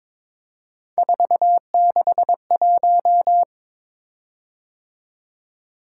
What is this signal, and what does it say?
Morse code, heard as a keyed tone, about 22 wpm, 700 Hz. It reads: 461
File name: practice / 22wpm / Three Letter-Number Combinations - 22wpm